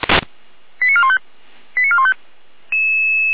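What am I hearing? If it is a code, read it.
Five-tone selective call (ZVEI), heard as quick runs of five tones, tones of about 70 ms. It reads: 89415 89415
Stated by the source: ZVEI five-tone selective call 88179 received on SDR.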